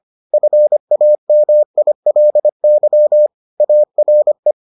Morse code, heard as a keyed tone, about 25 wpm, 600 Hz. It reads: FAMILY ARE